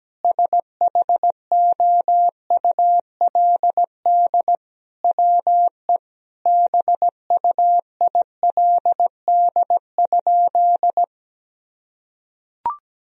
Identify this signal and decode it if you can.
Morse code, heard as a keyed tone, about 17 wpm, 700 Hz. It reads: SHOULD WE BUILD?
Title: Sentences from Top 1000 Words - Review Part 3 - 17wpm